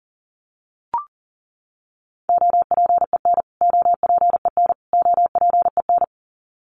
Morse code, heard as a keyed tone, about 40 wpm, 700 Hz. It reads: OPEN OPEN OPEN
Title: Top 300 Words Repeated 40wpm